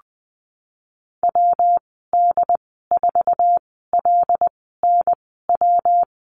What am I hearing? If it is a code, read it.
Morse code, heard as a keyed tone, about 20 wpm, spaced as for 15 wpm, 700 Hz. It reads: WD4LNW
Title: 186 Lesson 30g - Letter b - Call Signs 20wpm